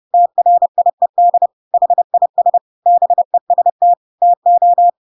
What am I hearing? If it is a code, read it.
Morse code, heard as a keyed tone, about 30 wpm, 700 Hz. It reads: TRIED HIS BEST TO
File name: Pinocchio / Pinocchio ch3 30 30 ik1hge